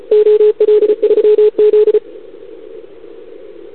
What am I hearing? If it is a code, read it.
Morse code, heard as a keyed tone, about 34 wpm, 440 Hz. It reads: OL3Z
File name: CQWWCW2021 / OL3Z